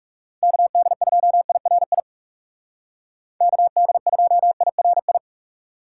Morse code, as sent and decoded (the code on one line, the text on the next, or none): -.- -.. ..--- .. .-. .. / -.- -.. ..--- .. .-. ..
KD2IRI KD2IRI